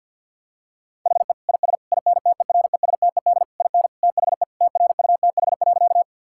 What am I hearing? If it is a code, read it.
Morse code, heard as a keyed tone, about 50 wpm, 700 Hz. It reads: HE IS INTERESTED IN THE TRUTH.
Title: Sentences from Top 1000 Words - Review Part 6 - 50wpm